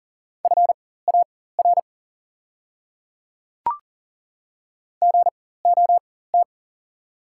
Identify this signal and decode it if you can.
Morse code, heard as a keyed tone, about 40 wpm, spaced as for 19 wpm, 690 Hz. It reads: FAR GOT
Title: 4x Character Spacing - Top 300 Words - 40wpm